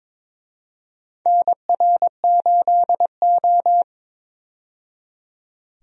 Morse code, heard as a keyed tone, about 22 wpm, 700 Hz. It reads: NR8O
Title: USA Call Signs - 4 Characters - 22wpm